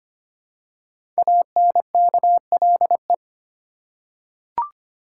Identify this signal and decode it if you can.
Morse code, heard as a keyed tone, about 25 wpm, 700 Hz. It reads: ANKLE
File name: 5 Letter Words - First 3 Letters Given - 25wpm